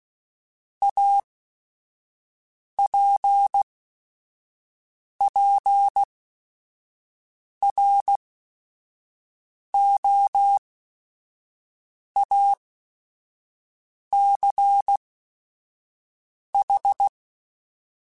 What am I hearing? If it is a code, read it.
Morse code, heard as a keyed tone, about 16 wpm, spaced as for 5 wpm, 800 Hz. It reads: APPROACH